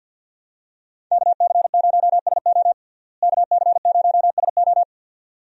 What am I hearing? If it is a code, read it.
Morse code, heard as a keyed tone, about 46 wpm, 700 Hz. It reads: KC0SO KC0SO
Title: USA Call Signs - 5 Characters - Repeated 2x - 50wpm